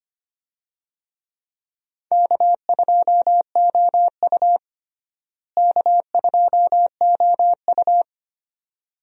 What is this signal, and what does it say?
Morse code, heard as a keyed tone, about 25 wpm, 700 Hz. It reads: K2OU K2OU
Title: USA Call Signs - 4 Characters - Repeated 2x - 25wpm